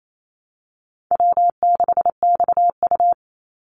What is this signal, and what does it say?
Morse code, heard as a keyed tone, about 28 wpm, 700 Hz. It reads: W6XU
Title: Easy Call Signs - 28wpm